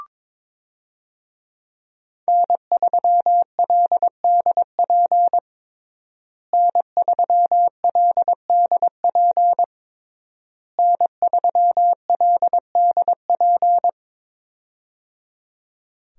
Morse code, heard as a keyed tone, about 22 wpm, 700 Hz. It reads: N3LDP N3LDP N3LDP
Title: USA Call Signs - 5 Characters - Repeated 3x - 22wpm